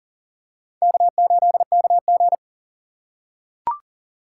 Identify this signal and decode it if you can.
Morse code, heard as a keyed tone, about 40 wpm, 700 Hz. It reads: K8KG
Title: USA Call Signs - 4 Characters - 2x Character Spacing - 40wpm